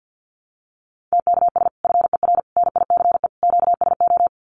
Morse code, heard as a keyed tone, about 50 wpm, 700 Hz. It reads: TKS FER NICE QSO
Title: Common QSO Phrases 50wpm